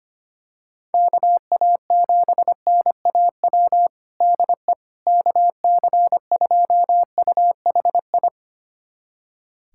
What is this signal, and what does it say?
Morse code, heard as a keyed tone, about 25 wpm, 700 Hz. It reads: KA7NAW DE KC2UHI
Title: USA Callsign DE Callsign 25wpm